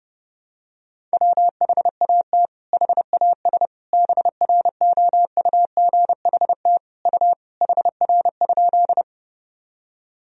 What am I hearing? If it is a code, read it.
Morse code, heard as a keyed tone, about 30 wpm, 700 Hz. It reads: WHAT HAS BROUGHT U HR?
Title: Sentences from Top 500 Words and Abbreviations 30wpm